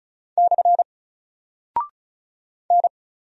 Morse code, heard as a keyed tone, about 36 wpm, 700 Hz. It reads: / N
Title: Single Character (Letter:Number:Punctuation:Prosign) 35wpm